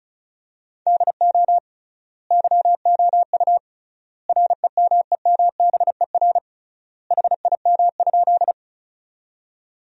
Morse code, heard as a keyed tone, about 36 wpm, 700 Hz. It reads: DO YOU REMEMBER HIM?
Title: 3x Word-Spacing - Sentences from Top 400 Words - 35wpm